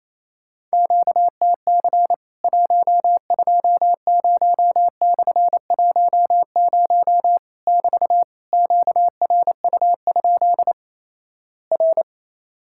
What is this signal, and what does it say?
Morse code, heard as a keyed tone, about 28 wpm, 700 Hz. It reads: QTC 120/10 <BT> QRU? R
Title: Worked All Europe DX Contest - QTC Exchange - 28wpm